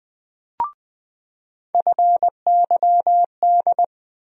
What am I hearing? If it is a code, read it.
Morse code, heard as a keyed tone, about 20 wpm, 700 Hz. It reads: FYD
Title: USA Call Sign Postfixes - 3 Letters 20wpm